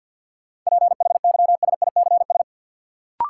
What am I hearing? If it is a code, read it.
Morse code, heard as a keyed tone, about 53 wpm, 700 Hz. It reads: PHYSICS
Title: Sentences from Top 1500 Words - New Word Drill Alt - 50wpm